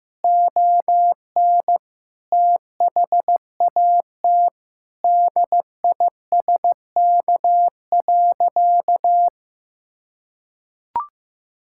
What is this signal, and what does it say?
Morse code, heard as a keyed tone, about 15 wpm, 700 Hz. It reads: ON THAT DISK.
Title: Sentences from Top 1000 Words - Review Part 6 - 15wpm